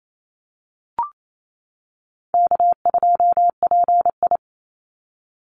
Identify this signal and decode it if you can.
Morse code, heard as a keyed tone, about 28 wpm, 700 Hz. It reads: K2PI
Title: Call Signs - 28wpm